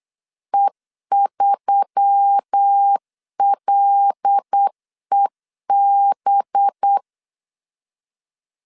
Morse code, heard as a keyed tone, about 8 wpm, 800 Hz. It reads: E3LEB
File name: CW / DET / 1 6 1500C S B 8wpm 5142